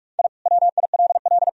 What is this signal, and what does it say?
Morse code, heard as a keyed tone, about 46 wpm, 700 Hz. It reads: I WILL
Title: Sentences from Top 1000 Words - Review Part 4 - 45wpm